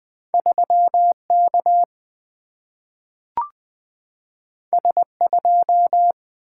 Morse code, heard as a keyed tone, about 20 wpm, 700 Hz. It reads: 3K S2
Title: Single-Letter Single-Number 20wpm